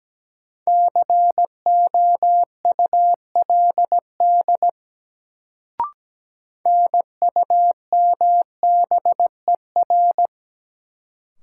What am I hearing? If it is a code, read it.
Morse code, heard as a keyed tone, about 17 wpm, 700 Hz. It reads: COULD NUMBER
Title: Sets of 2 Words - New Word Drill - 17wpm